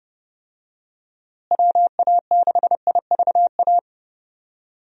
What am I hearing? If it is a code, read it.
Morse code, heard as a keyed tone, about 30 wpm, 700 Hz. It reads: WA6IVA
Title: USA Call Signs - 6 Characters - Repeated 3x - 30wpm